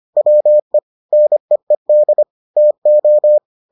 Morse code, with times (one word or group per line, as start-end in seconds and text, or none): 0.2-0.8 WE
1.1-2.2 NEED
2.6-3.4 TO